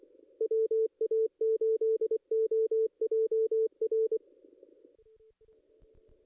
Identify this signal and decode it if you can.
Morse code, heard as a keyed tone, about 28 wpm, 440 Hz. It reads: WA8OJR
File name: contests / ss-cw-k5tr-op-n5rz -2200